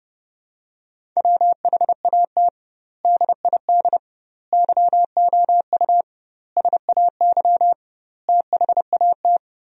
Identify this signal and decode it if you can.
Morse code, heard as a keyed tone, about 30 wpm, 700 Hz. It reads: WHAT DID YOU SAY THAT
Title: 2x Word-Spacing - Sentences from Top 200 Words - 30wpm